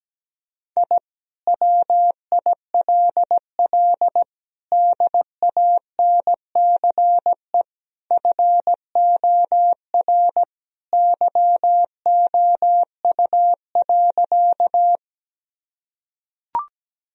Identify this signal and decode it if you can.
Morse code, heard as a keyed tone, about 17 wpm, 700 Hz. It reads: I WILL DANCE FOR YOU.
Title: Sentences from Top 1000 Words - Review Part 4 - 17wpm